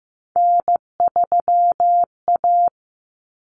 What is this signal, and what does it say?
Morse code, heard as a keyed tone, about 15 wpm, 700 Hz. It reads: N3A
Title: Common Cut Numbers - 3-Digit 15wpm